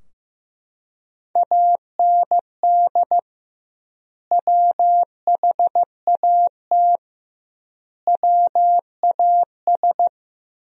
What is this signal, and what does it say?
Morse code, heard as a keyed tone, about 15 wpm, 700 Hz. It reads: AND WHAT WAS
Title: 2x Word-Spacing - Sentences from Top 100 Words - 15wpm